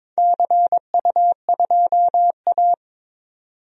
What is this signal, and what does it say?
Morse code, heard as a keyed tone, about 21 wpm, 700 Hz. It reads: CU2A
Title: DXCC-100 Call Signs - 4 Characters - 22wpm